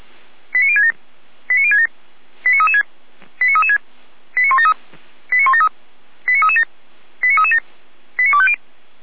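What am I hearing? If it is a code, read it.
Five-tone selective call (ZVEI), heard as quick runs of five tones, tones of about 70 ms. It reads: 89067 89067 89336 89336 89173 89173 89337 89337 89250
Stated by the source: ZVEI five-tone call received by SDR.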